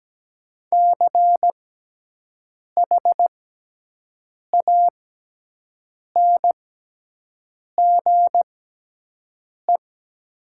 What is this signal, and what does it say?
Morse code, heard as a keyed tone, about 17 wpm, spaced as for 6 wpm, 700 Hz. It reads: CHANGE
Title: 6x Character Spacing - Top 200 Words - 17wpm